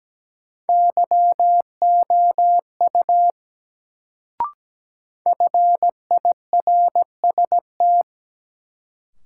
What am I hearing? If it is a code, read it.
Morse code, heard as a keyed tone, about 17 wpm, 700 Hz. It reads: YOU FIRST
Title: Sentences from Top 100 Words - New Word Drill - 17wpm